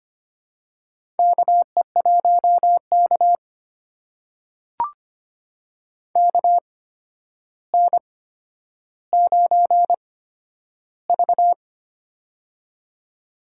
Morse code, heard as a keyed tone, about 25 wpm, spaced as for 11 wpm, 700 Hz. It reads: KE1K KN9V
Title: USA Call Signs - 4 Characters - 8x Character Spacing - 25wpm